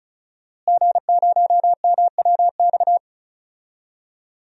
Morse code, heard as a keyed tone, about 34 wpm, 700 Hz. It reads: G0MWX
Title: DXCC-100 Call Signs - 5 Characters - Repeated 2x - 35wpm